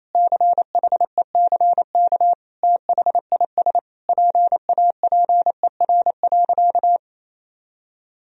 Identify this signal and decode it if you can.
Morse code, heard as a keyed tone, about 27 wpm, 700 Hz. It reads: CHECK THIS PAPER.
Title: Sentences from Top 1000 Words - Review Part 3 - 28wpm